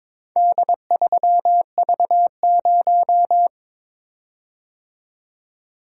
Morse code, heard as a keyed tone, about 22 wpm, 700 Hz. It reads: D3V0
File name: practice / 22wpm / Four Letter-Number Combinations - 22wpm